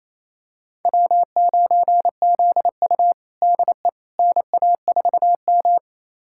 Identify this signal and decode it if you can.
Morse code, heard as a keyed tone, about 28 wpm, 700 Hz. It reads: W9ZU DE NA4M
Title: USA Callsign DE Callsign - 4 Characters - 28wpm